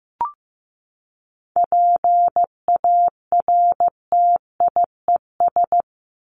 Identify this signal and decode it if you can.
Morse code, heard as a keyed tone, about 15 wpm, 700 Hz. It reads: PARTIES
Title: ES:S Suffix - Top 100 Words with Base Words 15wpm